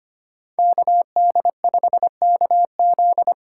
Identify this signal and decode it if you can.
Morse code, heard as a keyed tone, about 25 wpm, 700 Hz. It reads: KD5KZ
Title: USA Call Signs - 5 Characters - 2x Character Spacing - 25wpm